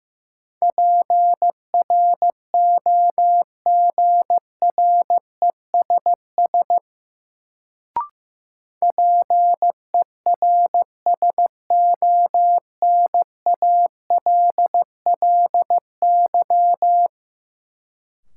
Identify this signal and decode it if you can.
Morse code, heard as a keyed tone, about 16 wpm, 700 Hz. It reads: PROGRESS PERSONALLY
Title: Sentences from Top 1400 Words - New Word Drill - 15wpm